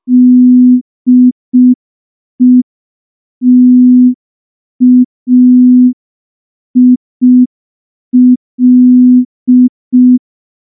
Morse code, heard as a keyed tone, about 5 wpm, 250 Hz. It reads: DETAIL
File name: morse / morse-training-05WPM